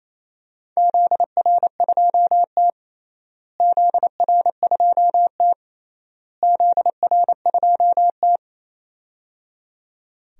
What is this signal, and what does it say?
Morse code, heard as a keyed tone, about 28 wpm, 700 Hz. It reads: ZR2T ZR2T ZR2T
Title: DXCC-100 Call Signs - 4 Characters - Repeated 3x - 28wpm